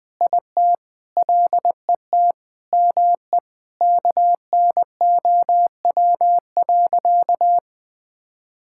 Morse code, heard as a keyed tone, about 20 wpm, 700 Hz. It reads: IT LET ME KNOW.
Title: Sentences with 8-Word Phrases - 20wpm